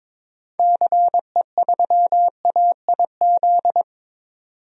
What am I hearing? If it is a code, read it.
Morse code, heard as a keyed tone, about 22 wpm, 700 Hz. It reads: CE3AIZ
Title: DXCC-100 Call Signs - 6 Characters - 22wpm